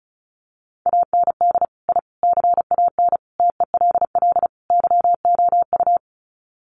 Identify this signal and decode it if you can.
Morse code, heard as a keyed tone, about 34 wpm, 700 Hz. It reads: AND I CAN TELL YOU
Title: Sets of 5 Words Encore2 35wpm